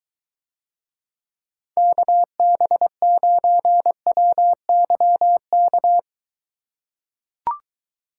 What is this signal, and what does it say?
Morse code, heard as a keyed tone, about 23 wpm, 700 Hz. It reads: KB9WYK